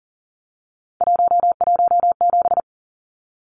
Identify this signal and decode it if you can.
Morse code, heard as a keyed tone, about 40 wpm, 700 Hz. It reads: 117
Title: Numbers - 3-Digit 40wpm